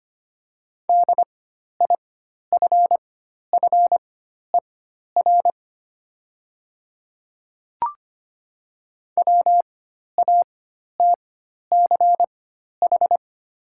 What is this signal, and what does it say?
Morse code, heard as a keyed tone, about 25 wpm, spaced as for 12 wpm, 690 Hz. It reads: DIFFER WATCH
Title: 4x Character Spacing - Top 300 Words - 25wpm